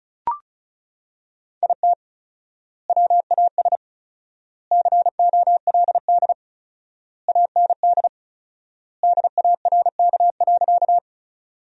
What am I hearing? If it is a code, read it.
Morse code, heard as a keyed tone, about 36 wpm, 700 Hz. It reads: IT WAS COLD AND DARK.
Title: 4x Word-Spacing - Sentences from Top 500 Words - 35wpm